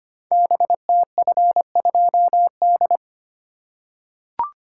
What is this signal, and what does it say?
Morse code, heard as a keyed tone, about 25 wpm, 700 Hz. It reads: BTF2D
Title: Code Groups - 5 Letter:Numbers - 25wpm